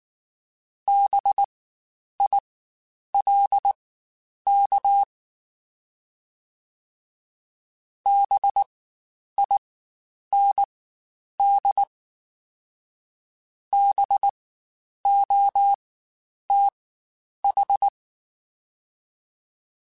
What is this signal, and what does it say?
Morse code, heard as a keyed tone, about 19 wpm, spaced as for 9 wpm, 800 Hz. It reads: BILK BIND BOTH
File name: Spaced Alphas / Spaced Alphas mp3s / Bs sp @18